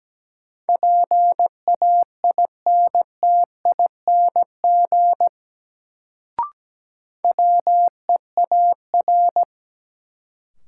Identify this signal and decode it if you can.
Morse code, heard as a keyed tone, about 17 wpm, 700 Hz. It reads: PAINTING WEAR
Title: Sentences from Top 1000 Words - Review Part 6 - New Word Drill - 17wpm